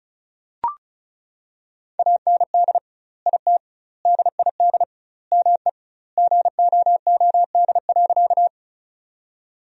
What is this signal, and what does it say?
Morse code, heard as a keyed tone, about 34 wpm, 700 Hz. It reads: AND IT DID ME GOOD.
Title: 2x Word-Spacing - Sentences from Top 200 Words - 35wpm